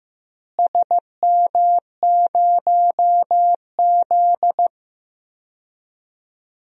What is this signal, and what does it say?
Morse code, heard as a keyed tone, about 15 wpm, 700 Hz. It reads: SM0Z
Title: DXCC-100 Call Signs - 4 Characters - Repeated 3x - 15wpm